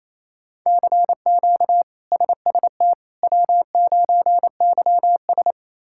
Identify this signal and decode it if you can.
Morse code, heard as a keyed tone, about 28 wpm, 700 Hz. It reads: CQ SST W9YS
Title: K1USN SST QSO Practice - 28wpm